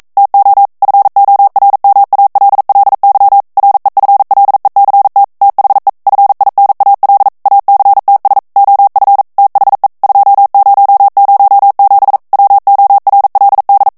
Morse code, heard as a keyed tone, about 42 wpm, 800 Hz. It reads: TO FORMALLY REFLECT THE FINAL ACTS OF THE 2007 WORLD